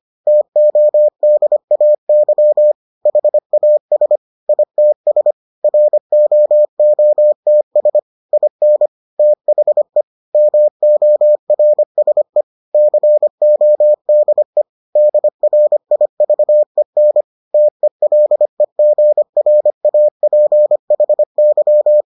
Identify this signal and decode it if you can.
Morse code, heard as a keyed tone, about 25 wpm, 600 Hz. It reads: TODAY HAS ITS ROOTS IN THE MORSE CODE DRIVEN TELEGRAPHY